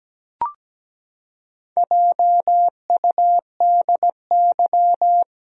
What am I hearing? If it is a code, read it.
Morse code, heard as a keyed tone, about 17 wpm, 700 Hz. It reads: JUDY
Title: USA Common Names - Max 4 Letters - 17wpm